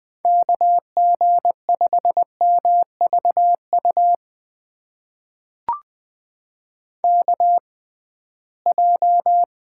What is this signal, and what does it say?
Morse code, heard as a keyed tone, about 20 wpm, spaced as for 15 wpm, 700 Hz. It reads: KG5MVU KJ
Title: USA Call Signs - 6x Character Spacing - 20wpm